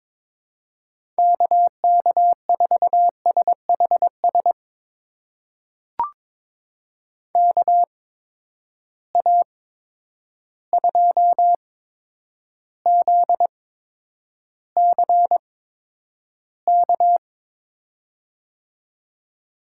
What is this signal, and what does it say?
Morse code, heard as a keyed tone, about 23 wpm, spaced as for 10 wpm, 700 Hz. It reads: KK4SHS KA2ZCK
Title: USA Call Signs - 6 Characters - 8x Character Spacing - 22wpm